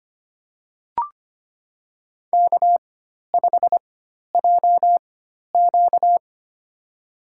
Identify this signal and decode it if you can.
Morse code, heard as a keyed tone, about 25 wpm, spaced as for 12 wpm, 690 Hz. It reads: K5JQ
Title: USA Call Signs - 4 Characters - 4x Character Spacing - 25wpm